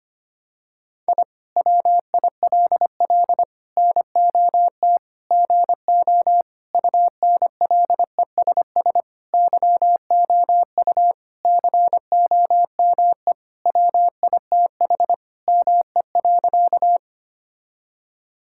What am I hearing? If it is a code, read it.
Morse code, heard as a keyed tone, about 25 wpm, 700 Hz. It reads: I WILL NOT GO UNLESS YOU COME WITH ME.
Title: Sentences from Top 1000 Words - Review Part 6 - 25wpm